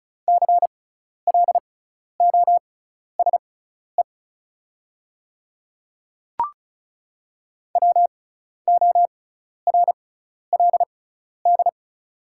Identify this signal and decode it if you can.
Morse code, heard as a keyed tone, about 36 wpm, spaced as for 12 wpm, 690 Hz. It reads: CLOSE WORLD
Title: 6x Character Spacing - Top 300 Words - 35wpm